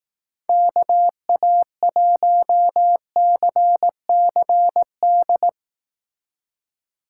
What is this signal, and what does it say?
Morse code, heard as a keyed tone, about 19 wpm, 700 Hz. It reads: KA1CCD
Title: Speed Racing - USA 2x3 Call Signs 15wpm